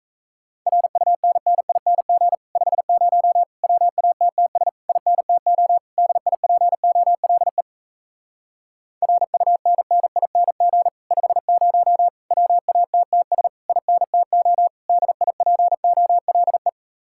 Morse code, heard as a keyed tone, about 38 wpm, 700 Hz. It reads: RUNNING 50 WATTS INTO DIPOLE RUNNING 50 WATTS INTO DIPOLE